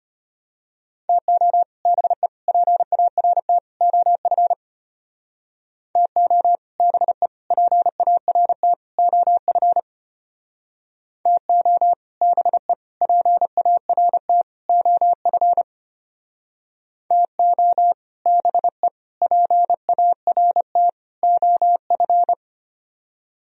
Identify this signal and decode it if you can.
Morse code, heard as a keyed tone, about 30 wpm, 700 Hz. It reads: TO BE PART OF TO BE PART OF TO BE PART OF TO BE PART OF
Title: Speed-Racing - Sets of 4 Words 25wpm